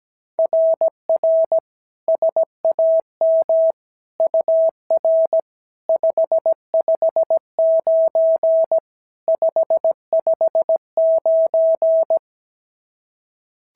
Morse code, heard as a keyed tone, about 16 wpm, 650 Hz. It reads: RR SAM UR 559 559
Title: SOTA QSO Practice 17wpm